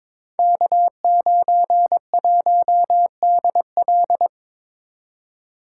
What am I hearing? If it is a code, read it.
Morse code, heard as a keyed tone, about 23 wpm, 700 Hz. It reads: K91DL
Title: Code Groups - 5 Letter:Numbers - 22wpm